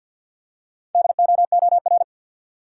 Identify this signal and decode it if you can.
Morse code, heard as a keyed tone, about 52 wpm, 700 Hz. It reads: DOOR
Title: Four Letter Words - 50wpm